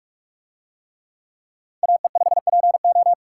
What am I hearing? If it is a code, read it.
Morse code, heard as a keyed tone, about 42 wpm, 700 Hz. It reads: AE5PO